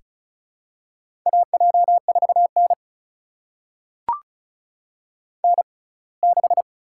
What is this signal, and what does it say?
Morse code, heard as a keyed tone, about 36 wpm, spaced as for 24 wpm, 700 Hz. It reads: AJ4N N6
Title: USA Call Signs - 4 Characters - 6x Character Spacing - 35wpm